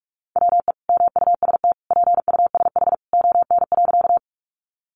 Morse code, heard as a keyed tone, about 44 wpm, 700 Hz. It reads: WE MUST PUSH ON.
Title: Sentences from Top 1300 Words 45wpm